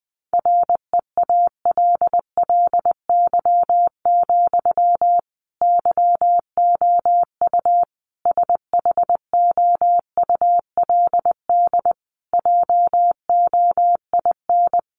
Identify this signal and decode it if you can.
Morse code, heard as a keyed tone, about 20 wpm, 700 Hz. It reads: REALLY, YOU SHOULD JOIN
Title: Sentences from Top 1400 Words 20wpm